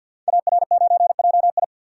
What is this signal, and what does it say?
Morse code, heard as a keyed tone, about 48 wpm, 700 Hz. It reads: AR9JI
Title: Code Groups - 5 Letter:Numbers - 50wpm